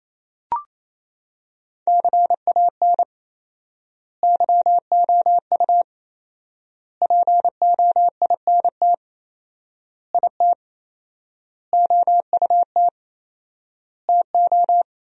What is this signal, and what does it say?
Morse code, heard as a keyed tone, about 28 wpm, 700 Hz. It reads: CAN YOU POINT IT OUT TO
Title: 4x Word-Spacing - Sentences from Top 200 Words - 28wpm